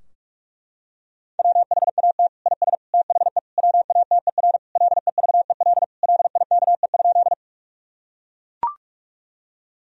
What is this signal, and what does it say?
Morse code, heard as a keyed tone, about 41 wpm, 700 Hz. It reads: WHAT IS THE WATER LEVEL LIKE?
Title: Sentences from Top 1000 Words - Review Part 4 - 45wpm